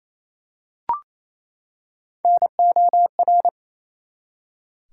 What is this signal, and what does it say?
Morse code, heard as a keyed tone, about 29 wpm, 700 Hz. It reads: NOR
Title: Three Letter Words - 28wpm